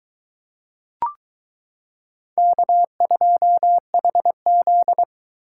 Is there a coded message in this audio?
Yes: Morse code from 2.4 to 5.0 s, heard as a keyed tone, about 22 wpm, 700 Hz. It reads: K2HZ